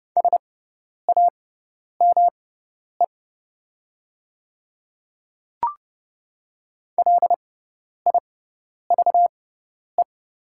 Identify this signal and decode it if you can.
Morse code, heard as a keyed tone, about 29 wpm, spaced as for 10 wpm, 700 Hz. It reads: SAME LIVE
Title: 6x Character Spacing - Top 500 Words - 30wpm